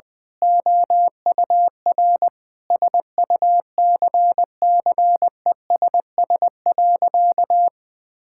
Morse code, heard as a keyed tone, about 21 wpm, 700 Hz. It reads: OUR SUCCESS.
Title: Sentences with 9-Word Phrases - 20wpm